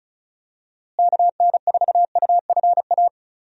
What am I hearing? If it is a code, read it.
Morse code, heard as a keyed tone, about 36 wpm, 700 Hz. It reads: KN4UFA
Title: USA Call Signs - 6 Characters - Repeated 3x - 35wpm